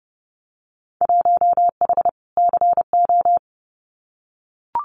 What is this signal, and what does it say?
Morse code, heard as a keyed tone, about 30 wpm, 700 Hz. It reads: 1H CO
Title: Winter Field Day Exchanges 30wpm